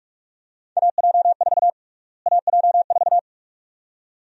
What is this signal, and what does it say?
Morse code, heard as a keyed tone, about 47 wpm, 700 Hz. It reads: AJ4 AJ4
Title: USA Call Sign Prefixes - Repeated 2x - 45wpm